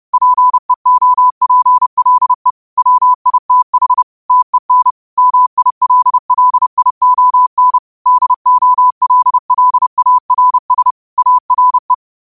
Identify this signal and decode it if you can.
Morse code, heard as a keyed tone, about 30 wpm, 1000 Hz. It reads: PEOPLE WITH TEN MILLION DOLLARS ARE
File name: qotdfiles / 30wpm